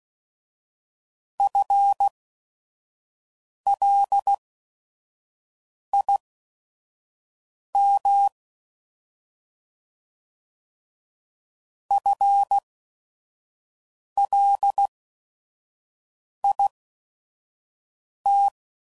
Morse code, heard as a keyed tone, about 16 wpm, spaced as for 5 wpm, 800 Hz. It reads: FLIM FLIT
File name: Prefixes / By First Letter / Fs